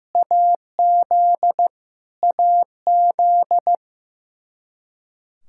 Morse code, heard as a keyed tone, about 15 wpm, 700 Hz. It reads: AZ AZ
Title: Practice QSO - Exchange 1 - Beginning - 15wpm